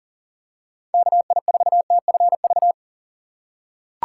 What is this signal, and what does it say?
Morse code, heard as a keyed tone, about 40 wpm, 700 Hz. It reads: KI4TFV
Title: USA Call Signs - 6 Characters - 3x Character Spacing - 40wpm